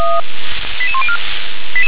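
Five-tone selective call (ZVEI), heard as quick runs of five tones, tones of about 70 ms. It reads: 88104
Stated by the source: ZVEI five-tone selective call received by SDR.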